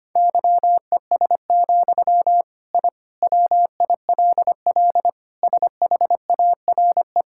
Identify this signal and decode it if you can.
Morse code, heard as a keyed tone, about 25 wpm, 700 Hz. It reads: YES, I WILL SHARE
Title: Sentences from Top 1000 Words - Review Part 6 - 25wpm